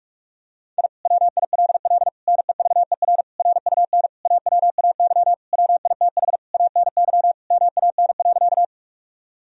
Morse code, heard as a keyed tone, about 41 wpm, 700 Hz. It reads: I WILL NEVER RUN AWAY WITH ANY MAN.
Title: Sentences from Top 1000 Words - Review Part 1 - 45wpm